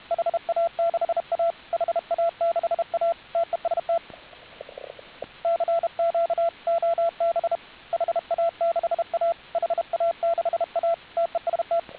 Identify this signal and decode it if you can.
Morse code, heard as a keyed tone, about 31 wpm, 670 Hz, with radio noise. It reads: HA6A HA6A TEST CQ OB HA6A HA6A TEST